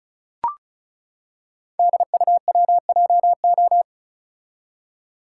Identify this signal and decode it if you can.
Morse code, heard as a keyed tone, about 36 wpm, 690 Hz. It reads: DUWJO
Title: Code Groups - 5 Letter:Numbers - 35wpm